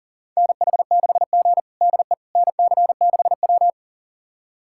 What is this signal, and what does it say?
Morse code, heard as a keyed tone, about 40 wpm, 700 Hz. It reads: NH6G DE NC6W
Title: USA Callsign DE Callsign - 4 Characters - 40wpm